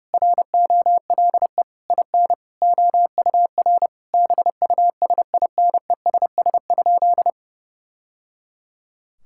Morse code, heard as a keyed tone, about 30 wpm, 700 Hz. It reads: ROLE IN OUR BUSINESS?
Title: Sentences from Top 1000 Words - Review Part 6 - 30wpm